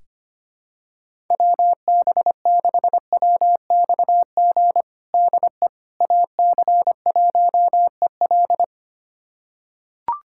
Morse code, heard as a keyed tone, about 25 wpm, 700 Hz. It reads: WB6WXG DE AC1EL
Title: USA Callsign DE Callsign 25wpm